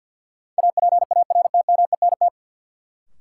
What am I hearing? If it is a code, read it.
Morse code, heard as a keyed tone, about 52 wpm, 700 Hz. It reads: APARTMENT
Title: Sentences from Top 900 Words - New Word Drill - 50wpm